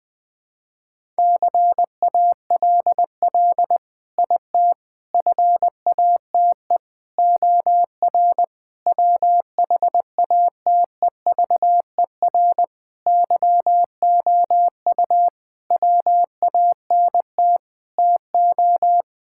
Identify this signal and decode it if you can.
Morse code, heard as a keyed tone, about 20 wpm, 700 Hz. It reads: CALL IT FATE OR WHATEVER YOU WANT TO
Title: Sentences with 7-Word Phrases - 20wpm